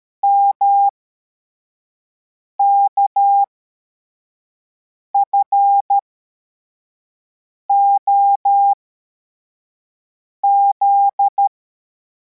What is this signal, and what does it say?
Morse code, heard as a keyed tone, about 13 wpm, spaced as for 4 wpm, 800 Hz. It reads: MKFOZ
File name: formation / cw / Text81